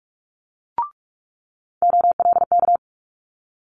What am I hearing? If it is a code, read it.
Morse code, heard as a keyed tone, about 46 wpm, 700 Hz. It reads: OLK
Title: Multi-letter Phonemes Quick Fire 45wpm